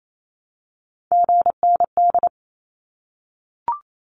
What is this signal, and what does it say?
Morse code, heard as a keyed tone, about 28 wpm, 700 Hz. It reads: GND
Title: Abbreviations 28wpm